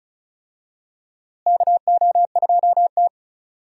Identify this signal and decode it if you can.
Morse code, heard as a keyed tone, about 36 wpm, 700 Hz. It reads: KO2T
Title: USA Call Signs - 4 Characters - Repeated 3x - 35wpm